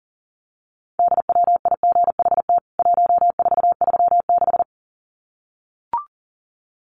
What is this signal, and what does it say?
Morse code, heard as a keyed tone, about 40 wpm, 700 Hz. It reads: DWIGHT 1436
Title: CWT Exchange 40wpm